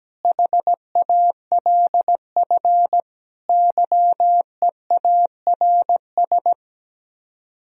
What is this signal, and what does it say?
Morse code, heard as a keyed tone, about 17 wpm, 700 Hz. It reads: HALF YEARS
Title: Speed-Racing - Sets of 5 Words 17wpm